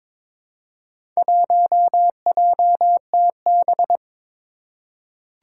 Morse code, heard as a keyed tone, about 22 wpm, 700 Hz. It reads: 1JTB
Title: Four Challenging Letter-Number Combinations 22wpm